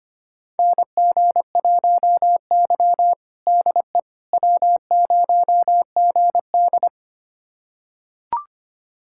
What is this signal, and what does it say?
Morse code, heard as a keyed tone, about 25 wpm, 700 Hz. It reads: NG1Y DE W0GD
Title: USA Callsign DE Callsign - 4 Characters - 25wpm